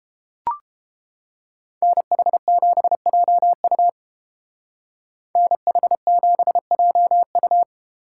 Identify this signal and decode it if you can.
Morse code, heard as a keyed tone, about 29 wpm, 700 Hz. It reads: NH7JU NH7JU